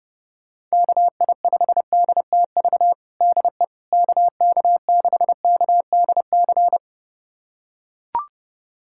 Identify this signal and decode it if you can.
Morse code, heard as a keyed tone, about 30 wpm, 690 Hz. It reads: KI5DTV DE KK6KDC
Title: USA Callsign DE Callsign - 6 Characters - 30wpm